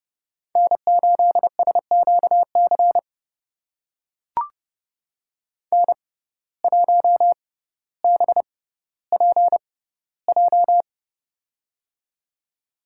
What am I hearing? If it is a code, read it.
Morse code, heard as a keyed tone, about 32 wpm, spaced as for 16 wpm, 700 Hz. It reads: N8SQC N1BPJ
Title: USA Call Signs - 5 Characters - 6x Character Spacing - 30wpm